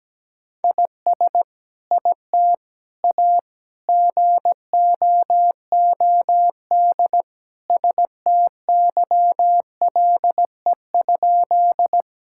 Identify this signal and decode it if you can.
Morse code, heard as a keyed tone, about 17 wpm, 700 Hz. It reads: IS IT A GOOD STYLE?
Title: Sentences from Top 1000 Words - Review Part 5 - 17wpm